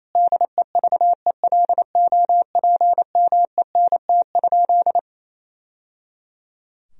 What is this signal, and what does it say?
Morse code, heard as a keyed tone, about 28 wpm, 700 Hz. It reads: DEVELOPMENT?
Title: Sentences from Top 1000 Words - Review Part 4 - 28wpm